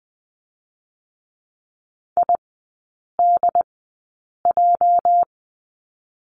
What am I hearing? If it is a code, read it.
Morse code, heard as a keyed tone, about 20 wpm, spaced as for 8 wpm, 700 Hz. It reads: IDJ
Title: 205 Lesson 33e - Letter j - Three Characters 20wpm